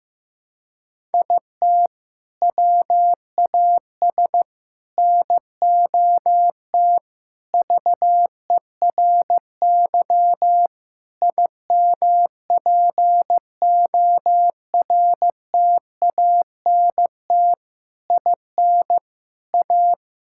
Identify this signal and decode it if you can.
Morse code, heard as a keyed tone, about 15 wpm, 700 Hz. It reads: IT WAS NOT VERY IMPORTANT IN A
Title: Sentences from Top 1000 Words - Review Part 5 - 15wpm